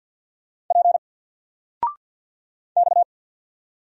Morse code, heard as a keyed tone, about 50 wpm, 700 Hz. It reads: P X
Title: Single Letter-Number 50wpm